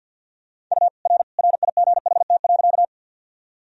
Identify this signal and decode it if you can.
Morse code, heard as a keyed tone, about 50 wpm, 700 Hz. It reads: U R RIGHT.
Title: Sentences from Top 200 Words and Abbreviations 50wpm